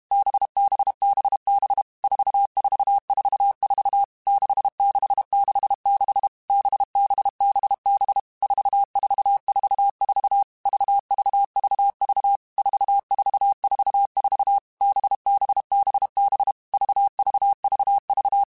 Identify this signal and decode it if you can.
Morse code, heard as a keyed tone, about 32 wpm, 800 Hz. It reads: BBBB 4444 6666 BBBB 4444 VVVV 4444 BBBB VVVV
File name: Groups@30 / Groups@30 mp3s / VB46@30